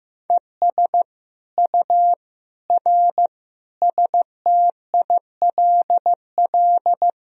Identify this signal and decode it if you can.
Morse code, heard as a keyed tone, about 16 wpm, 700 Hz. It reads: ES U R STILL
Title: Sentences from Top 400 Words and Abbreviations 15wpm